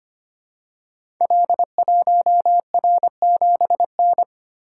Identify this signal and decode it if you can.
Morse code, heard as a keyed tone, about 25 wpm, 700 Hz. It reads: L1R7N
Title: Code Groups - 5 Letter:Numbers - 25wpm